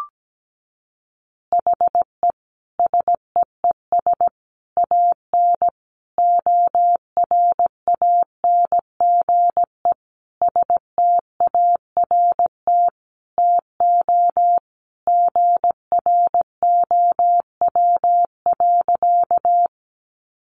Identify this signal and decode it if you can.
Morse code, heard as a keyed tone, about 17 wpm, 700 Hz. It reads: HE SEES AN ORANGE START TO GROW.
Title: Easy Phrases 17wpm